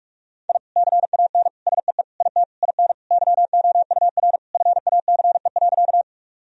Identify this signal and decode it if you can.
Morse code, heard as a keyed tone, about 44 wpm, 700 Hz. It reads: I CAN SEE IT IN YOUR FACE.
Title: Sentences from Top 1000 Words - Review Part 2 - 45wpm